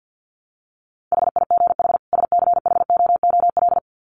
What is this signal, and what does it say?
Morse code, heard as a keyed tone, about 50 wpm, 700 Hz. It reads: HIGH SCHOOL
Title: Sets of 2 Words 50wpm